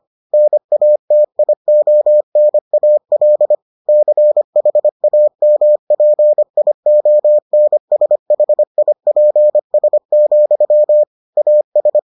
Morse code, heard as a keyed tone, about 25 wpm, 600 Hz. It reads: NATIONAL CHAMPIONSHIPS, AS